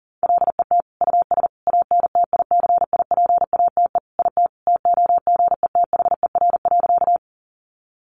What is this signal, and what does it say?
Morse code, heard as a keyed tone, about 40 wpm, 700 Hz. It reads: LET US ANTICIPATE IT TOGETHER.
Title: Sentences from Top 1500 Words 40wpm